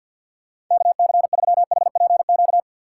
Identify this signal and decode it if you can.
Morse code, heard as a keyed tone, about 47 wpm, 700 Hz. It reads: KC3HPQ
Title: USA Call Signs - 6 Characters - 6x Character Spacing - 50wpm